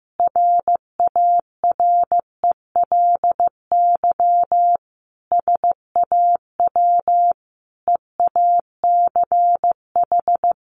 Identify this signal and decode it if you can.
Morse code, heard as a keyed tone, about 15 wpm, 700 Hz. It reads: RARELY SAW EACH
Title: Sentences from Top 1400 Words 15wpm